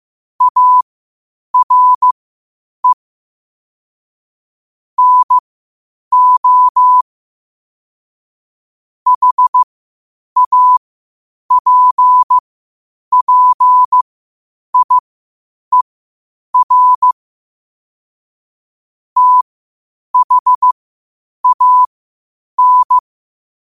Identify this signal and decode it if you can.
Morse code, heard as a keyed tone, about 15 wpm, spaced as for 8 wpm, 1000 Hz. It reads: ARE NO HAPPIER THAN
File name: qotdfiles / 8wpm